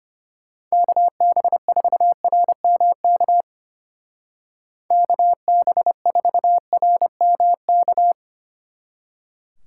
Morse code, heard as a keyed tone, about 31 wpm, 700 Hz. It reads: KB4RMK KB4RMK
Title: Speed Racing - USA 2x3 Call Signs 25wpm